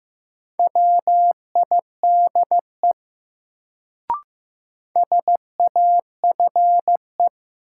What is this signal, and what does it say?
Morse code, heard as a keyed tone, about 15 wpm, 700 Hz. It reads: WIDE SAFE
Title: Sentences from Top 900 Words - New Word Drill - 15wpm